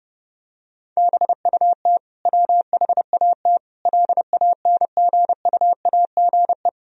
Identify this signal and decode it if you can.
Morse code, heard as a keyed tone, about 30 wpm, 700 Hz. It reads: BUT WHAT LANGUAGE
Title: Sentences from Top 1000 Words - Review Part 3 - 30wpm